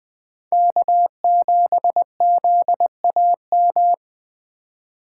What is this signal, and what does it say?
Morse code, heard as a keyed tone, about 21 wpm, 700 Hz. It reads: K7ZAM
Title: USA Call Signs - 5 Characters - 4x Character Spacing - 20wpm